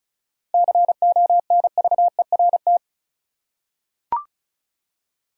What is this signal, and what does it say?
Morse code, heard as a keyed tone, about 34 wpm, 700 Hz. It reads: CONVERT
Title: Seven Letter Words - 35wpm